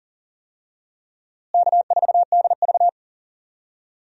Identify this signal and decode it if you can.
Morse code, heard as a keyed tone, about 37 wpm, 700 Hz. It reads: K4DV
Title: USA Call Signs - 4 Characters - Repeated 2x - 40wpm